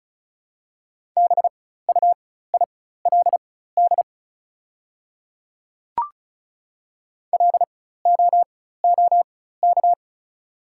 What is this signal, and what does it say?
Morse code, heard as a keyed tone, about 34 wpm, spaced as for 16 wpm, 690 Hz. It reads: BUILD LOOK
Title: 4x Character Spacing - Top 200 Words - 35wpm